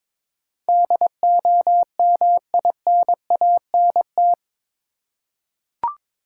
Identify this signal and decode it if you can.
Morse code, heard as a keyed tone, about 22 wpm, 700 Hz. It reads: DOMINANT
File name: practice / 22wpm / Eight Letter Words - 22wpm